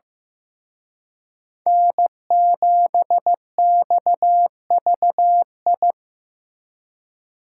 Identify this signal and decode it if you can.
Morse code, heard as a keyed tone, about 18 wpm, 700 Hz. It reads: N7XVI
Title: Speed Racing - USA 1x3 Call Signs 15wpm